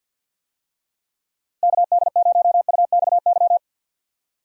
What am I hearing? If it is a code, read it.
Morse code, heard as a keyed tone, about 53 wpm, 700 Hz. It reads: KD0UXY
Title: USA Call Signs - 6 Characters - Repeated 3x - 50wpm